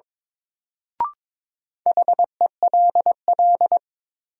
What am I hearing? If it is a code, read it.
Morse code, heard as a keyed tone, about 22 wpm, 700 Hz. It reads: HELL
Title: Sets of 5 Words Encore - New Word Drill - 22wpm